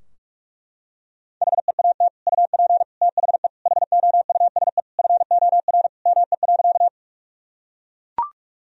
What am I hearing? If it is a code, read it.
Morse code, heard as a keyed tone, about 42 wpm, 700 Hz. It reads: HEAT UP THE HOUSE FOR ME.
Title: Sentences from Top 1000 Words - Review Part 3 - 45wpm